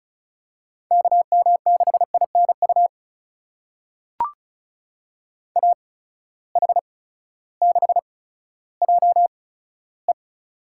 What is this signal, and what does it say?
Morse code, heard as a keyed tone, about 36 wpm, spaced as for 16 wpm, 700 Hz. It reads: KM6INU AH6JE
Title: USA Call Signs - 8x Character Spacing - 35wpm